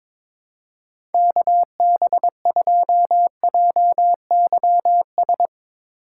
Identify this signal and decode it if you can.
Morse code, heard as a keyed tone, about 23 wpm, 700 Hz. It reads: KB2JYS
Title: USA Call Signs - 6 Characters - 4x Character Spacing - 22wpm